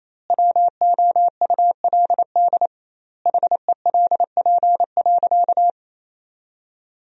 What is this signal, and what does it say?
Morse code, heard as a keyed tone, about 28 wpm, 700 Hz. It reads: WOULD HELP.
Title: 2x Word-Spacing - Sentences from Top 200 Words - 28wpm